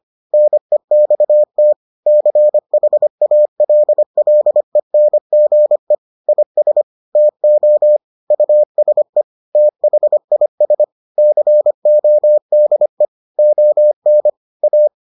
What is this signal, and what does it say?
Morse code, heard as a keyed tone, about 25 wpm, 600 Hz. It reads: NEXT CHALLENGE IS TO USE THIS CODE ON A